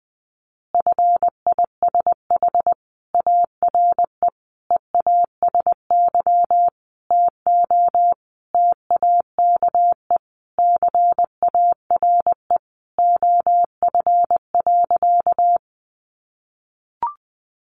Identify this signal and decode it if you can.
Morse code, heard as a keyed tone, about 20 wpm, 700 Hz. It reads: FISH ARE EASY TO TAKE CARE OF.
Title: Additional Easy Phrases 20wpm